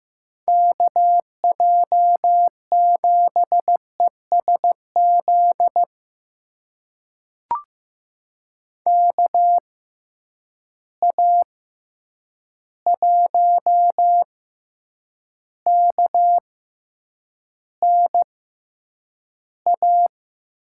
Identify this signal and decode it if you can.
Morse code, heard as a keyed tone, about 15 wpm, spaced as for 8 wpm, 700 Hz. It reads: KJ7ESZ KA1KNA
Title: USA Call Signs - 6 Characters - 6x Character Spacing - 15wpm